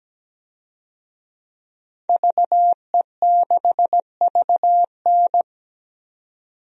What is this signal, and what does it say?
Morse code, heard as a keyed tone, about 17 wpm, 700 Hz. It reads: VE6VN
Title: DXCC-100 Call Signs - 5 Characters - 17wpm